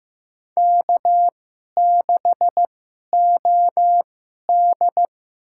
Morse code, heard as a keyed tone, about 15 wpm, spaced as for 11 wpm, 700 Hz. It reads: K6OD
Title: USA Call Signs - 4 Characters - 2x Character Spacing - 15wpm